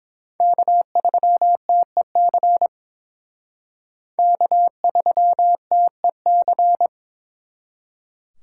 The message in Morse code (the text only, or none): K3TEC K3TEC